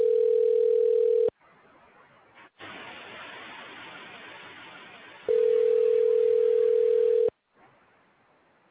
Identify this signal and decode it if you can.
Ringback tone (North America), 440 + 480 Hz.
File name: recordings / RE73b898a2dbb20fd670d1b1f c